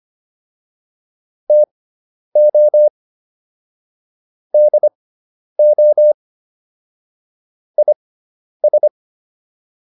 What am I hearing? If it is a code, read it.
Morse code, heard as a keyed tone, about 25 wpm, spaced as for 10 wpm, 600 Hz. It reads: TO DO IS